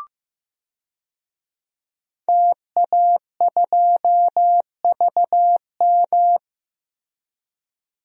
Morse code, heard as a keyed tone, about 15 wpm, 700 Hz. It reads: TA2VM
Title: DXCC-100 Call Signs - 5 Characters - Repeated 3x - 15wpm